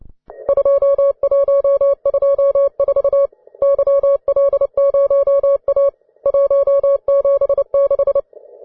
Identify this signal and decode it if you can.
Morse code, heard as a keyed tone, about 29 wpm, 560 Hz. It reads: 2124 YL0A 176
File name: wae / cw advanced / ve3kz cw